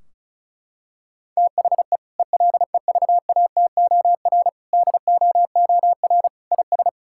Morse code, heard as a keyed tone, about 36 wpm, 700 Hz. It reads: THE ELEVATOR DOOR IS
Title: Sentences from Top 1600 Words 35wpm